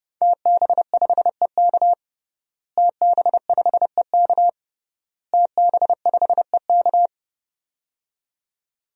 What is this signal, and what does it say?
Morse code, heard as a keyed tone, about 32 wpm, 700 Hz. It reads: TB5EK TB5EK TB5EK
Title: DXCC-100 Call Signs - 5 Characters - Repeated 3x - 30wpm